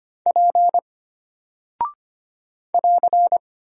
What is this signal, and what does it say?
Morse code, heard as a keyed tone, about 25 wpm, 700 Hz. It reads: P <AR>
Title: Single Character (Letter:Number:Punctuation:Prosign) Alt 25wpm